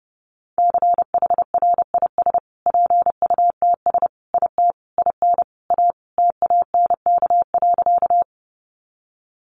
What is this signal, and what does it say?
Morse code, heard as a keyed tone, about 30 wpm, 700 Hz. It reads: CHRIS PUTS IT IN A TANK.
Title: Additional Easy Phrases 30wpm